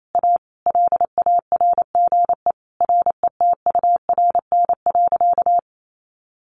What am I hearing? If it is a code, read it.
Morse code, heard as a keyed tone, about 28 wpm, 700 Hz. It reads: A LARGE RETURN.
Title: Sentences from Top 1100 Words 28wpm